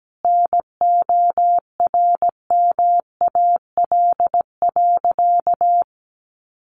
Morse code, heard as a keyed tone, about 17 wpm, 700 Hz. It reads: NORMAL.
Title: Sentences from Top 1200 Words 17wpm